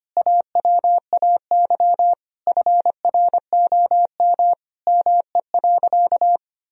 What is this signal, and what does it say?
Morse code, heard as a keyed tone, about 25 wpm, 700 Hz. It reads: AWAY FROM ME.
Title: Sentences from Top 1000 Words - Review Part 2 - 25wpm